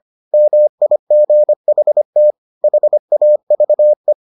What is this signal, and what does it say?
Morse code, heard as a keyed tone, about 25 wpm, 600 Hz. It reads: MIGHT HAVE